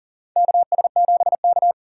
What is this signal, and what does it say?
Morse code, heard as a keyed tone, about 40 wpm, 700 Hz. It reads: KS7K
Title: USA Call Signs - 4 Characters - 4x Character Spacing - 40wpm